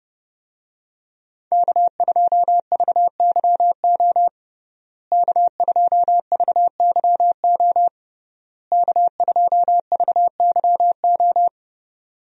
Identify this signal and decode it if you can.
Morse code, heard as a keyed tone, about 30 wpm, 700 Hz. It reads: K2VYO K2VYO K2VYO
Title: USA Call Signs - 5 Characters - Repeated 3x - 30wpm